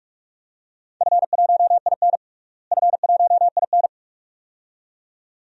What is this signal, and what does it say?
Morse code, heard as a keyed tone, about 43 wpm, 700 Hz. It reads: F1IN F1IN
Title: DXCC-100 Call Signs - 4 Characters - Repeated 2x - 45wpm